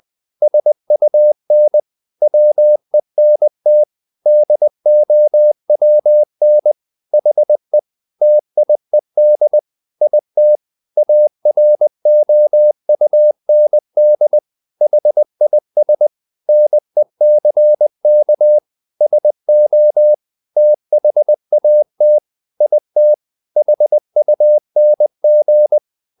Morse code, heard as a keyed tone, about 20 wpm, 600 Hz. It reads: SUN WENT DOWN HE TIED IT AROUND HIS NECK SO THAT IT HUNG
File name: the old man and the sea / the old man and the sea part 1 20wpm